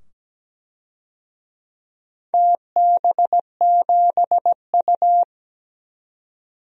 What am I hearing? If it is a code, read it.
Morse code, heard as a keyed tone, about 17 wpm, 700 Hz. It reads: TB7U
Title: DXCC-100 Call Signs - 4 Characters - 17wpm